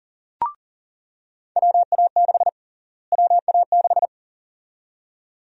Morse code, heard as a keyed tone, about 40 wpm, 700 Hz. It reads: WA6 WA6
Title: USA Call Sign Prefixes - Repeated 2x - 40wpm